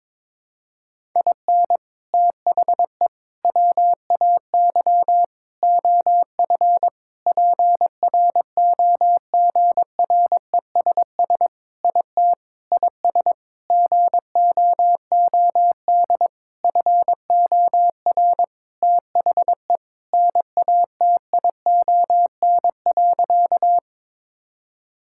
Morse code, heard as a keyed tone, about 22 wpm, 700 Hz. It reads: IN THE WAY OF PROGRESS IT IS GOOD FOR THE NATION.
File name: practice / 22wpm / Sentences with 4-Word Phrases - 22wpm